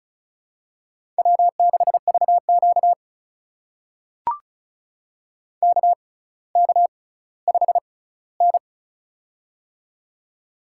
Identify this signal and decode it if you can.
Morse code, heard as a keyed tone, about 34 wpm, spaced as for 18 wpm, 700 Hz. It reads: W6VQ KK5N
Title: USA Call Signs - 4 Characters - 6x Character Spacing - 35wpm